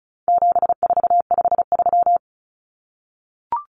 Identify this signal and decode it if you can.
Morse code, heard as a keyed tone, about 34 wpm, 700 Hz. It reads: 7453
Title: Numbers - 4-Digit 35wpm